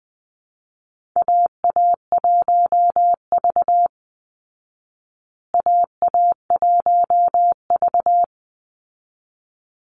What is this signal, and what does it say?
Morse code, heard as a keyed tone, about 20 wpm, 700 Hz. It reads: AA1V AA1V
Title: Call Signs Repeated 3x - 20wpm